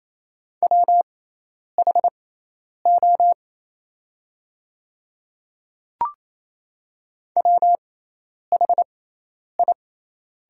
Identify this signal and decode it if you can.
Morse code, heard as a keyed tone, about 28 wpm, spaced as for 10 wpm, 700 Hz. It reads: WHO WHI
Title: 6x Character Spacing - Top 100 Words - 28wpm